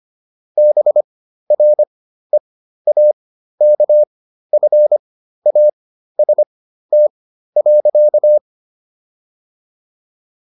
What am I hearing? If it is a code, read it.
Morse code, heard as a keyed tone, about 25 wpm, spaced as for 13 wpm, 600 Hz. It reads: BREAKFAST.